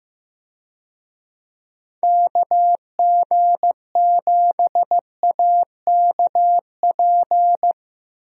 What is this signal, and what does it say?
Morse code, heard as a keyed tone, about 15 wpm, 700 Hz. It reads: KG7AKP
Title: USA Call Signs - 6 Characters - 15wpm